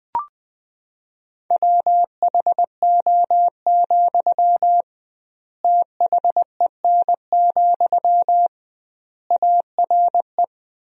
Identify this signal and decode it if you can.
Morse code, heard as a keyed tone, about 20 wpm, 700 Hz. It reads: WHO, THEN, ARE
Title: 2x Word-Spacing - Sentences from Top 100 Words - 20wpm